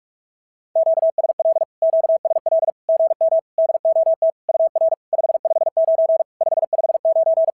Morse code, heard as a keyed tone, about 44 wpm, 690 Hz. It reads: QSL QSL GM DOT UR 559 559
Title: SOTA QSO Practice 45wpm